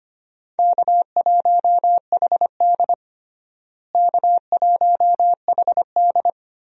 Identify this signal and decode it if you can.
Morse code, heard as a keyed tone, about 25 wpm, 700 Hz. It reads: K1HD K1HD
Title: USA Call Signs - 4 Characters - Repeated 3x - 25wpm